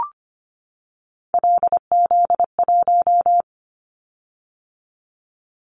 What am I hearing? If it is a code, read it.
Morse code, heard as a keyed tone, about 25 wpm, 700 Hz. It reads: LZ1
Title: Callsign Prefixes for Top 100 DXCC - 25wpm